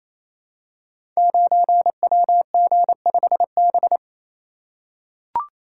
Morse code, heard as a keyed tone, about 28 wpm, 700 Hz. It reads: 9WG5B
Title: Code Groups - 5 Letter:Numbers - 28wpm